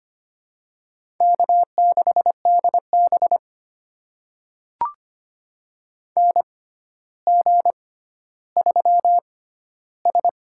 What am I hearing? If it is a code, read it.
Morse code, heard as a keyed tone, about 25 wpm, spaced as for 13 wpm, 700 Hz. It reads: K6DB NG3S
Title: USA Call Signs - 4 Characters - 6x Character Spacing - 25wpm